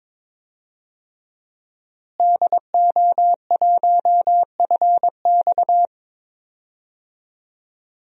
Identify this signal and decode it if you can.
Morse code, heard as a keyed tone, about 22 wpm, 700 Hz. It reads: DO1FX
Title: DXCC-100 Call Signs - 5 Characters - Repeated 3x - 22wpm